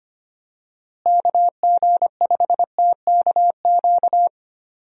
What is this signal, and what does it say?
Morse code, heard as a keyed tone, about 25 wpm, 700 Hz. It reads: KG5TKQ
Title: USA Call Signs - 6 Characters - Repeated 2x - 25wpm